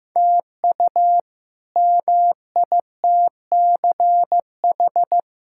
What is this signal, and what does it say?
Morse code, heard as a keyed tone, about 15 wpm, 700 Hz. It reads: TU MITCH
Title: K1USN SST QSO Practice - 15wpm